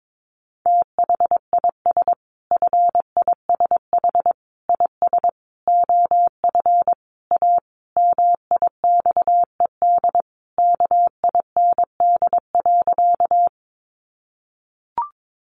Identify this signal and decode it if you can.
Morse code, heard as a keyed tone, about 22 wpm, 700 Hz. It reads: THIS FISH IS OF A MIXED KIND.
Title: Sentences from Top 1500 Words 22wpm